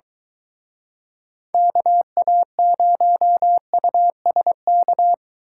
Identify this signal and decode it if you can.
Morse code, heard as a keyed tone, about 23 wpm, 700 Hz. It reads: KA0USK